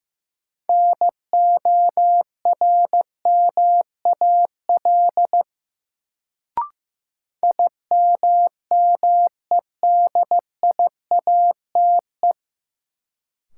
Morse code, heard as a keyed tone, about 16 wpm, 700 Hz. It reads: NORMAL IMMEDIATE
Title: Sentences from Top 1200 Words - New Word Drill - 15wpm